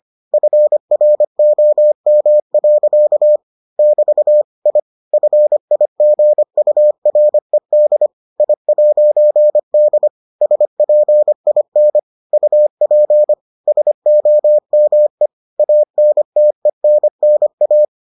Morse code, heard as a keyed tone, about 25 wpm, 600 Hz. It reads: FROM. <BT> I FIGURED I'D SPIN UP SOME ANTENNA